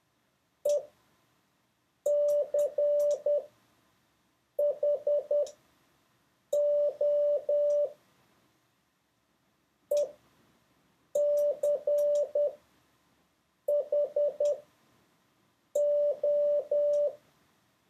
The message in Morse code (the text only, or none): ECHO ECHO